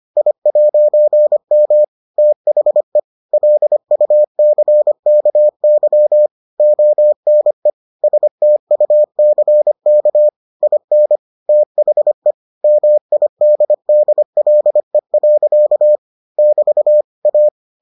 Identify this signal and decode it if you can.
Morse code, heard as a keyed tone, about 25 wpm, 600 Hz. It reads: I'M THE LUCKY ONE STUCK IN THE MIDDLE. <BT> A